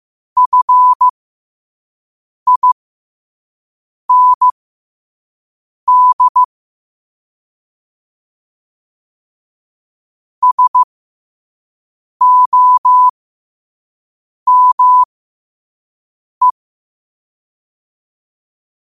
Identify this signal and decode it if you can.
Morse code, heard as a keyed tone, about 15 wpm, spaced as for 5 wpm, 1000 Hz. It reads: FIND SOME
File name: qotdfiles / 5wpm